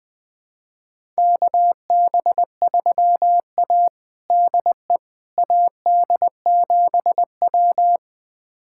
Morse code, heard as a keyed tone, about 21 wpm, 700 Hz. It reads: KB3A DE AD7W
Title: USA Callsign DE Callsign - 4 Characters - 20wpm